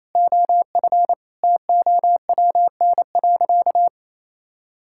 Morse code, heard as a keyed tone, about 28 wpm, 700 Hz. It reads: OF TOWN.
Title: Sentences from Top 1600 Words 28wpm